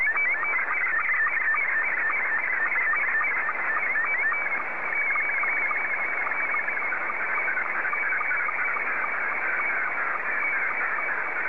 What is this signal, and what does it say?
SSTV (slow-scan television), Scottie 2.